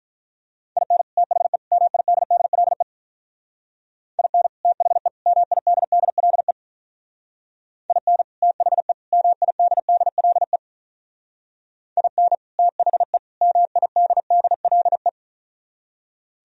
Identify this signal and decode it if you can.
Morse code, heard as a keyed tone, about 43 wpm, 700 Hz. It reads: IN THE MIDDLE IN THE MIDDLE IN THE MIDDLE IN THE MIDDLE
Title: Speed-Racing - Sets of 3 Words 35wpm